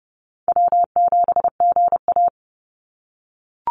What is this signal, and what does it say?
Morse code, heard as a keyed tone, about 32 wpm, 700 Hz. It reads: W7GA
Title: NA SOTA Chaser Callsigns 30wpm